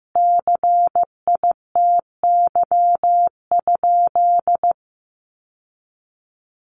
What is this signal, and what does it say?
Morse code, heard as a keyed tone, about 15 wpm, 700 Hz. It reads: CITY?
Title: Sentences from Top 1300 Words 15wpm